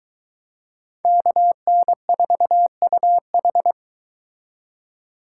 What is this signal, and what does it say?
Morse code, heard as a keyed tone, about 24 wpm, 700 Hz. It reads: KN4UH